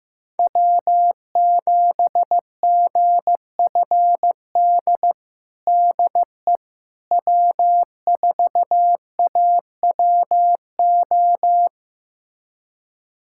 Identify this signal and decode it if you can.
Morse code, heard as a keyed tone, about 15 wpm, 700 Hz. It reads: W7GFD DE W4AWO
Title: USA Callsign DE Callsign - 5 Characters - 15wpm